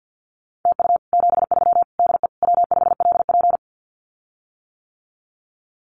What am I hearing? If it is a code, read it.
Morse code, heard as a keyed tone, about 50 wpm, 700 Hz. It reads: TU 73 DE W5LP
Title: Common QSO Phrases Encore 50wpm